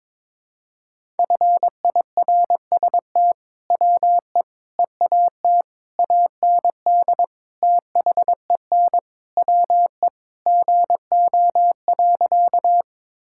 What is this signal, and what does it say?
Morse code, heard as a keyed tone, about 22 wpm, 700 Hz. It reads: FIRST WE EAT AND THEN WE GO.
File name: practice / 22wpm / Sentences with 2-Word Phrases - 22wpm